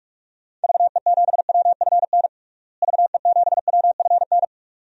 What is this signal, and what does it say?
Morse code, heard as a keyed tone, about 42 wpm, 700 Hz. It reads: VE7WFN VE7WFN
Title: DXCC-100 Call Signs - 6 Characters - Repeated 2x - 45wpm